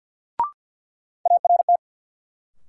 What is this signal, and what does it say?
Morse code, heard as a keyed tone, about 53 wpm, 700 Hz. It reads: ART
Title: Sentences from Top 600 Words - New Word Drill - 50wpm